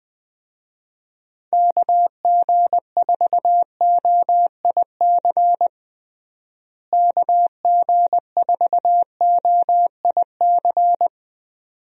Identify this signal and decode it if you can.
Morse code, heard as a keyed tone, about 20 wpm, 700 Hz. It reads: KG4OIC KG4OIC
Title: USA Call Signs - 6 Characters - Repeated 2x - 20wpm